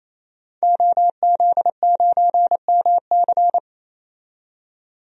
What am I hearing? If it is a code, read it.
Morse code, heard as a keyed tone, about 28 wpm, 700 Hz. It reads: OZ9MC
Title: DXCC-100 Call Signs - 5 Characters - 28wpm